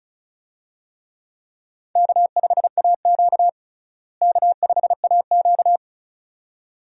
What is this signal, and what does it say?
Morse code, heard as a keyed tone, about 36 wpm, 700 Hz. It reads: K5AQ K5AQ
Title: USA Call Signs - 4 Characters - Repeated 2x - 35wpm